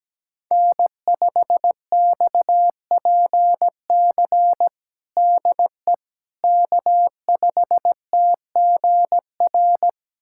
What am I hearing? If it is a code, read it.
Morse code, heard as a keyed tone, about 17 wpm, 700 Hz. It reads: N5XPC DE K5TGR
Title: USA Callsign DE Callsign - 5 Characters - 17wpm